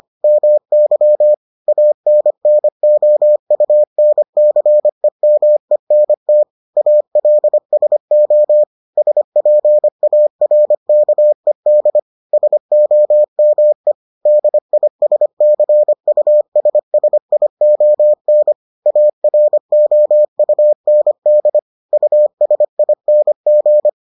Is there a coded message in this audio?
Yes: Morse code from 0.2 to 23.9 s, heard as a keyed tone, about 25 wpm, 600 Hz. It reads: MY ANNOUNCEMENT ALSO SPARKED SOME DISCUSSION AROUND USING